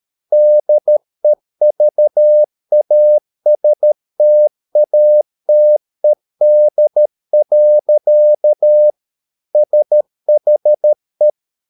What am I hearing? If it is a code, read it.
Morse code, heard as a keyed tone, about 13 wpm, 600 Hz. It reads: DEVASTATED. SHE